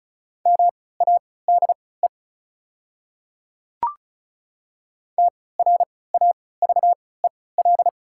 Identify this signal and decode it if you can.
Morse code, heard as a keyed tone, about 36 wpm, spaced as for 20 wpm, 690 Hz. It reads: MADE TRAVEL
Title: 3x Character Spacing - Top 500 Words - 35wpm